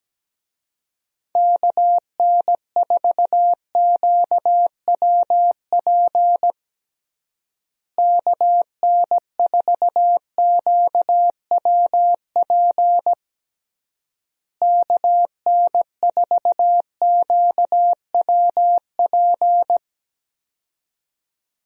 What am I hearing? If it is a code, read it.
Morse code, heard as a keyed tone, about 17 wpm, 700 Hz. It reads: KN4QWP KN4QWP KN4QWP
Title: USA Call Signs - 6 Characters - Repeated 3x - 17wpm